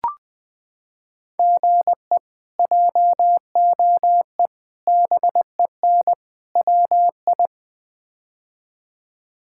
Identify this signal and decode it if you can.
Morse code, heard as a keyed tone, about 20 wpm, 700 Hz. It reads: GE JOE BEN WI
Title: K1USN SST QSO Practice - 20wpm